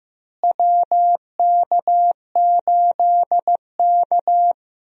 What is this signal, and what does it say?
Morse code, heard as a keyed tone, about 15 wpm, 700 Hz. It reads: WK8K
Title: USA Call Signs - 4 Characters - Repeated 2x - 15wpm